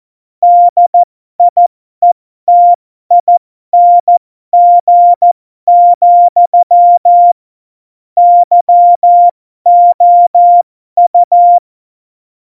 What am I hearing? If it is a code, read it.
Morse code, heard as a keyed tone, about 14 wpm, spaced as for 12 wpm, 700 Hz. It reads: DIETING, YOU
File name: Morsekurs / Witze